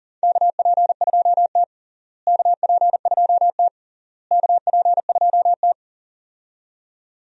Morse code, heard as a keyed tone, about 40 wpm, 700 Hz. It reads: KP2T KP2T KP2T
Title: USA Call Signs - 4 Characters - Repeated 3x - 40wpm